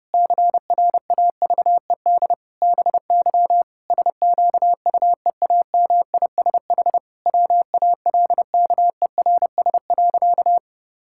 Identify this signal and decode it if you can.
Morse code, heard as a keyed tone, about 32 wpm, 700 Hz. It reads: CRAVED BY SQUEAMISH WALKERS.
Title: Pangram Challenge v2 - 30wpm